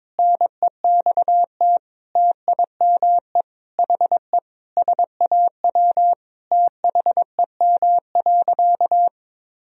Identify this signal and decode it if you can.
Morse code, heard as a keyed tone, about 22 wpm, 700 Hz. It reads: NEXT TIME HE SAW THEM.
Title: Sentences from Top 1000 Words - Review Part 1 - 22wpm